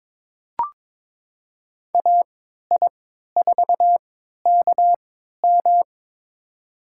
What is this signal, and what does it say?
Morse code, heard as a keyed tone, about 22 wpm, spaced as for 13 wpm, 700 Hz. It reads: AI4KM
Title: USA Call Signs - 5 Characters - 3x Character Spacing - 22wpm